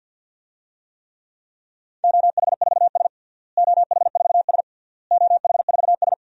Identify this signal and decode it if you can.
Morse code, heard as a keyed tone, about 49 wpm, 700 Hz. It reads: OH4S OH4S OH4S
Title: DXCC-100 Call Signs - 4 Characters - Repeated 3x - 50wpm